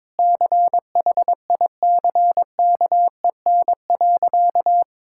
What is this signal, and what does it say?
Morse code, heard as a keyed tone, about 21 wpm, 700 Hz. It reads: CHICKEN.
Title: Sentences from Top 1000 Words - Review Part 5 - 22wpm